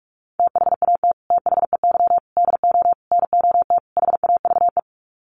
Morse code, heard as a keyed tone, about 44 wpm, 700 Hz. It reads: THAT THEY DO NOT HAVE
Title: Sets of 5 Words Encore2 45wpm